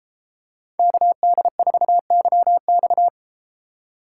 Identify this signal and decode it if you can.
Morse code, heard as a keyed tone, about 32 wpm, 700 Hz. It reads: KD4YX